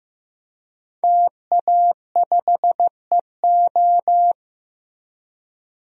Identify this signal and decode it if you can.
Morse code, heard as a keyed tone, about 15 wpm, 700 Hz. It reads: TA5EO
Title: DXCC-100 Call Signs - 5 Characters - Repeated 2x - 15wpm